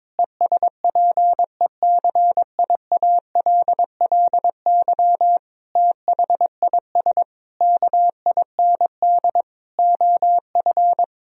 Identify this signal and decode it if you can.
Morse code, heard as a keyed tone, about 23 wpm, 700 Hz. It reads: ESPECIALLY THIS KIND OF
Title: Sentences from Top 1000 Words - Review Part 4 - 22wpm